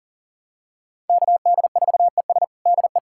Morse code, heard as a keyed tone, about 37 wpm, 700 Hz. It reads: KD4ES DE
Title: USA Callsign DE Callsign - 5 Characters - 40wpm